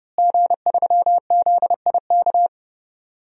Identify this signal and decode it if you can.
Morse code, heard as a keyed tone, about 30 wpm, 700 Hz. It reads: G3ZIK
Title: DXCC-100 Call Signs - 5 Characters - Repeated 2x - 30wpm